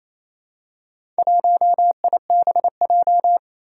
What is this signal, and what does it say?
Morse code, heard as a keyed tone, about 27 wpm, 700 Hz. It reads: 1IBJ
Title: Four Challenging Letter-Number Combinations 28wpm